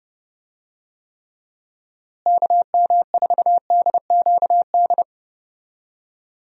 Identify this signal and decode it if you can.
Morse code, heard as a keyed tone, about 32 wpm, 700 Hz. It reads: KM4DQD
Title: USA Call Signs - 6 Characters - Repeated 3x - 30wpm